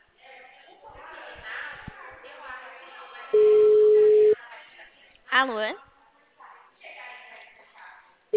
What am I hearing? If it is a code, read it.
Ringback tone (Europe, 425 Hz), with speech.
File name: out- -2044- - - .31745